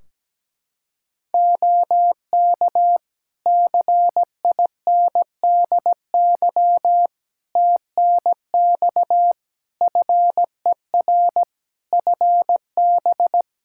Morse code, heard as a keyed tone, about 17 wpm, 700 Hz. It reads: OK CINDY TNX FER FB
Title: K1USN SST QSO Practice 17wpm